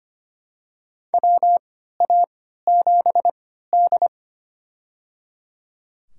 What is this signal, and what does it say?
Morse code, heard as a keyed tone, about 25 wpm, spaced as for 14 wpm, 700 Hz. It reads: WA7D
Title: USA Call Signs - 3x Character Spacing - 25wpm